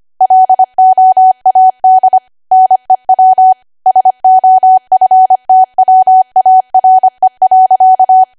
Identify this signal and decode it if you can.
Morse code, heard as a keyed tone, about 25 wpm, 750 Hz. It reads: LOAD NEW SOFTWARE.